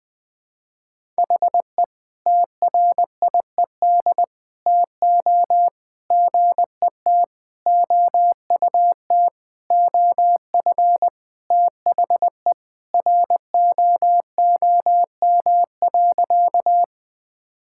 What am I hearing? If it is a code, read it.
Morse code, heard as a keyed tone, about 21 wpm, 700 Hz. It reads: HE TRIED TO GET OUT OF THE ROOM.
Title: Sentences with 5-Word Phrases - 20wpm